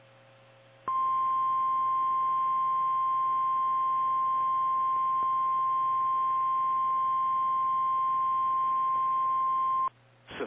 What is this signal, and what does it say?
Emergency Alert System (SAME) tones (the 1050 Hz attention tone). It is heard from 0.9 to 9.9 s.
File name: eas / KKPRKXPN / KKPRKXPN